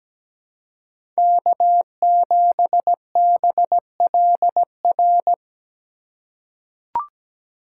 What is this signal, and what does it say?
Morse code, heard as a keyed tone, about 17 wpm, 700 Hz. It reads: K7BLR
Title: USA Call Signs - 5 Characters - 2x Character Spacing - 17wpm